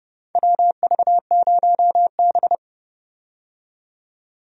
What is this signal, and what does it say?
Morse code, heard as a keyed tone, about 29 wpm, 700 Hz. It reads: WV0B
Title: USA Call Signs - 4 Characters - 30wpm